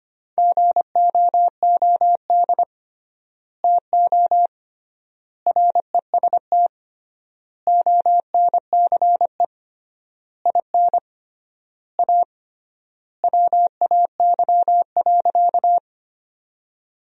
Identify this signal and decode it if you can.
Morse code, heard as a keyed tone, about 25 wpm, 700 Hz. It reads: GOOD TO REST ONCE IN A WAY.
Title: 3x Word-Spacing - Sentences from Top 500 Words - 25wpm